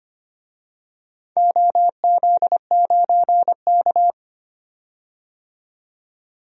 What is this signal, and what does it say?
Morse code, heard as a keyed tone, about 25 wpm, 700 Hz. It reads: OZ9K
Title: DXCC-100 Call Signs - 4 Characters - 25wpm